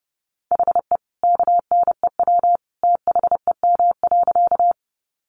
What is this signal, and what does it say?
Morse code, heard as a keyed tone, about 30 wpm, 700 Hz. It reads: HE KNEW THEM.
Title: Sentences from Top 400 Words 30wpm